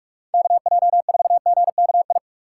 Morse code, heard as a keyed tone, about 46 wpm, 700 Hz. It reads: KJ4GKI
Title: USA Call Signs - 6 Characters - 2x Character Spacing - 45wpm